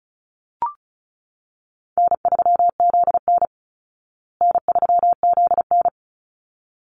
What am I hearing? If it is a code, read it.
Morse code, heard as a keyed tone, about 34 wpm, 700 Hz. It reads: N3ZN N3ZN
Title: Easy Call Signs Repeated 2x - 35wpm